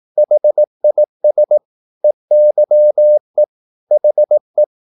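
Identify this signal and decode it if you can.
Morse code, heard as a keyed tone, about 18 wpm, 600 Hz. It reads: HIS EYE HE
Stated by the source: Morse practice at 18 wpm, English text, from CW med Gnister.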